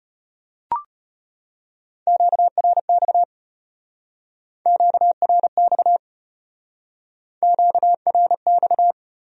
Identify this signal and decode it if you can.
Morse code, heard as a keyed tone, about 33 wpm, 700 Hz. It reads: QRX QRX QRX